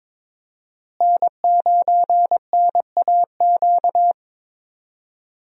Morse code, heard as a keyed tone, about 22 wpm, 700 Hz. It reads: N9NAQ
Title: USA Call Signs - 5 Characters - Repeated 3x - 22wpm